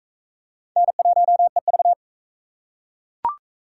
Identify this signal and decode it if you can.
Morse code, heard as a keyed tone, about 40 wpm, 700 Hz. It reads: N1EV